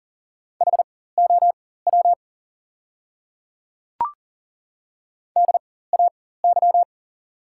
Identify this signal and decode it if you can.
Morse code, heard as a keyed tone, about 37 wpm, spaced as for 18 wpm, 690 Hz. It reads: HOW DAY
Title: 4x Character Spacing - Top 100 Words - 40wpm